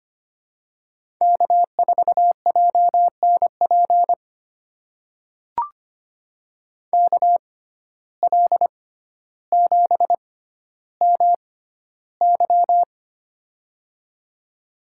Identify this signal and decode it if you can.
Morse code, heard as a keyed tone, about 25 wpm, spaced as for 13 wpm, 700 Hz. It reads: K4JNP KL7MY
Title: USA Call Signs - 5 Characters - 6x Character Spacing - 25wpm